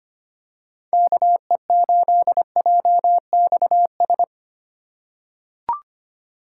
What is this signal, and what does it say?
Morse code, heard as a keyed tone, about 25 wpm, 700 Hz. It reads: KE8JXS
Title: USA Call Signs - 6 Characters - 3x Character Spacing - 25wpm